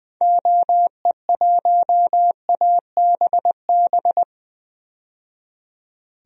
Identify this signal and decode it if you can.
Morse code, heard as a keyed tone, about 21 wpm, 700 Hz. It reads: OE1ABB
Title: DXCC-100 Call Signs - 6 Characters - 20wpm